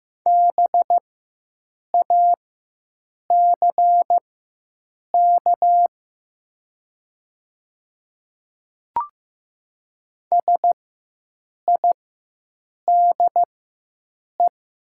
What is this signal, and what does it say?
Morse code, heard as a keyed tone, about 15 wpm, spaced as for 7 wpm, 700 Hz. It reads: BACK SIDE
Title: 4x Character Spacing - Top 200 Words - 15wpm